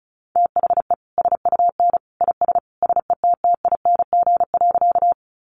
Morse code, heard as a keyed tone, about 34 wpm, 700 Hz. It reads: THE SUN IS SETTING.
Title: Sentences from Top 1300 Words 35wpm